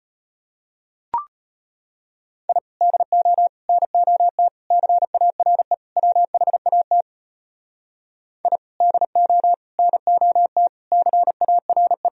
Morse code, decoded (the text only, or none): I DO NOT CARE WHAT I DO NOT CARE